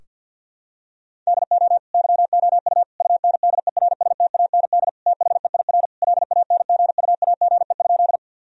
Morse code, heard as a keyed tone, about 50 wpm, 700 Hz. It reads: DO YOU UNDERSTAND THEIR LANGUAGE?
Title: Sentences from Top 1000 Words - Review Part 3 - 50wpm